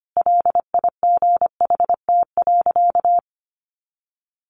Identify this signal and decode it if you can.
Morse code, heard as a keyed tone, about 25 wpm, 700 Hz. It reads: LIGHT.
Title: Sentences from Top 200 Words 25wpm